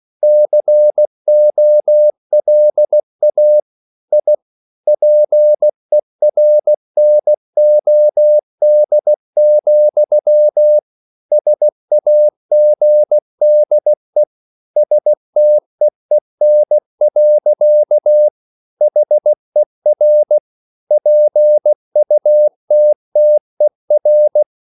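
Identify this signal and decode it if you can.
Morse code, heard as a keyed tone, about 16 wpm, 600 Hz. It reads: COLA I PERNOD, SAGDE STEEN. HER PUTTER
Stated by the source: Slow (QRS) Morse code at 16 wpm, from CW med Gnister.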